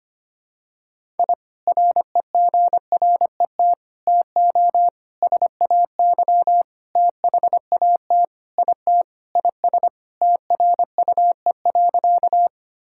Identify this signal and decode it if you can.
Morse code, heard as a keyed tone, about 25 wpm, 700 Hz. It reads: I REGRET TO SAY THAT IT IS TRUE.
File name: practice / 25wpm / Sentences with 3-Word Phrases - 25wpm